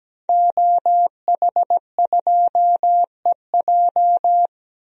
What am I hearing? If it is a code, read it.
Morse code, heard as a keyed tone, about 17 wpm, 700 Hz. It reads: OH2EJ
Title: DXCC-100 Call Signs - 5 Characters - Repeated 3x - 17wpm